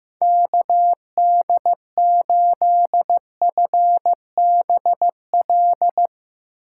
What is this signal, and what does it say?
Morse code, heard as a keyed tone, about 15 wpm, 700 Hz. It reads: KD8FBL
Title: USA Call Signs - 6 Characters - 8x Character Spacing - 15wpm